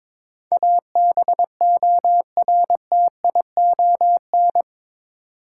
Morse code, heard as a keyed tone, about 22 wpm, 700 Hz. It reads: ABORTION
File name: practice / 22wpm / Eight Letter Words - 22wpm